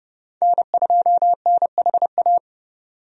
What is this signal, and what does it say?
Morse code, heard as a keyed tone, about 30 wpm, 700 Hz. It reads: N2NHA
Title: USA Call Signs - 5 Characters - 30wpm